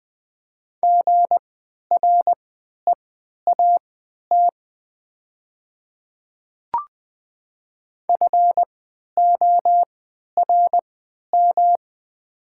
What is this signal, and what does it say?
Morse code, heard as a keyed tone, about 20 wpm, spaced as for 11 wpm, 700 Hz. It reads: GREAT FORM
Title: 3x Character Spacing - Top 300 Words - 20wpm